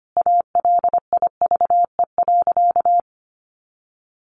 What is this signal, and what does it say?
Morse code, heard as a keyed tone, about 25 wpm, 700 Hz. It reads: ALIVE.
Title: A Sound Of Thunder 25wpm